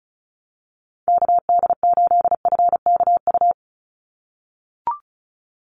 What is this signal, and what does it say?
Morse code, heard as a keyed tone, about 36 wpm, 700 Hz. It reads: KD8FKU
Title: USA Call Signs 35wpm